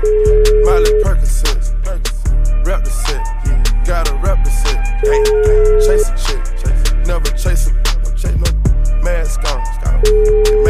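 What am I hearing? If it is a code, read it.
Ringback tone (Europe, 425 Hz).
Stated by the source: Caller ringback melody from O2 Czech Republic, heard by the caller before the call is answered.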